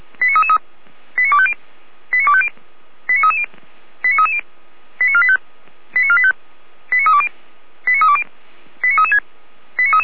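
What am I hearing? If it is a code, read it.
Five-tone selective call (ZVEI), heard as quick runs of five tones, tones of about 70 ms. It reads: 89250 89250 89330 89330 89485 89485 89320 89320 89336
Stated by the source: Received by SDR.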